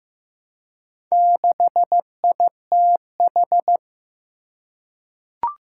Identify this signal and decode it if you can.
Morse code, heard as a keyed tone, about 15 wpm, 700 Hz. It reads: 6ITH
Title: Four Challenging Letter-Number Combinations 15wpm